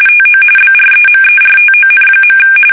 RTTY, about 45.45 baud. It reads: TO BREMERTON.
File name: fsk850 / battleshipnj